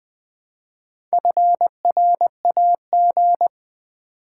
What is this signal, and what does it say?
Morse code, heard as a keyed tone, about 20 wpm, 700 Hz. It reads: FRAG
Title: Build A Word - 2 Letters At A Time - 8-Letter Words - 20wpm